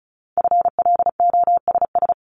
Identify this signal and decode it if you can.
Morse code, heard as a keyed tone, about 36 wpm, 700 Hz. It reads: FLOSS
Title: Multi-letter Phonemes 35wpm